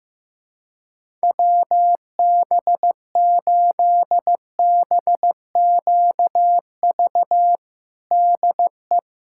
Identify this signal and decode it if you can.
Morse code, heard as a keyed tone, about 15 wpm, 700 Hz. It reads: WB8BQV DE
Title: USA Callsign DE Callsign 15wpm